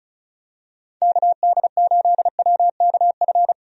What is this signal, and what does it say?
Morse code, heard as a keyed tone, about 36 wpm, 700 Hz. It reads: KD8WKF
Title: USA Call Signs - 6 Characters - Repeated 3x - 35wpm